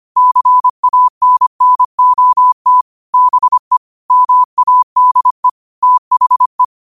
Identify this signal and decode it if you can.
Morse code, heard as a keyed tone, about 25 wpm, 1000 Hz. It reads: CANNOT BE MADE THE
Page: New quotes every day in morse code at 25 Words per minute.